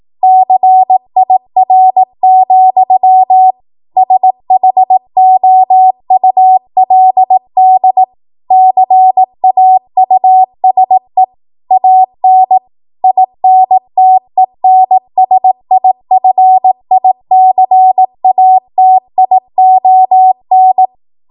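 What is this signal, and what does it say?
Morse code, heard as a keyed tone, about 18 wpm, 750 Hz. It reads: CIR, SHOULD CAUSE AN INTENSIFICATION
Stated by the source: 18 WPM Morse code practice from a national amateur radio association's code archive.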